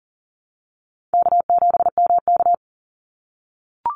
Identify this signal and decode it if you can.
Morse code, heard as a keyed tone, about 40 wpm, 700 Hz. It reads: K7MK
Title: NA SOTA Activator Callsigns 40wpm